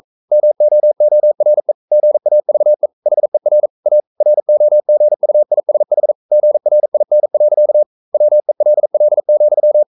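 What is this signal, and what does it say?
Morse code, heard as a keyed tone, about 42 wpm, 600 Hz. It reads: MOORE GAVE HER A ROGUISH GRIN. WELL,
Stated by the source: Morse code practice at 41 wpm, from CW med Gnister.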